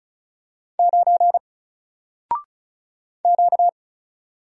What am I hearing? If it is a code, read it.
Morse code, heard as a keyed tone, about 36 wpm, 700 Hz. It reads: 9 Q
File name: Single-Letter-Number-35wpm